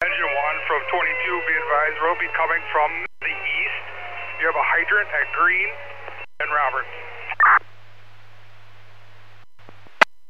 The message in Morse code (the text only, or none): none